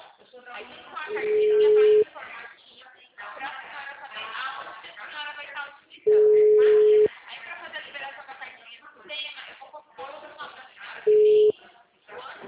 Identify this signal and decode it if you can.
Ringback tone (Europe, 425 Hz), with speech.